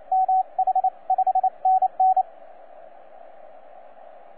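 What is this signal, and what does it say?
Morse code, heard as a keyed tone, about 28 wpm, 750 Hz, with radio noise. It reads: MH 5NN
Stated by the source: TZ6BB on 12 metres CW, from DX World.net.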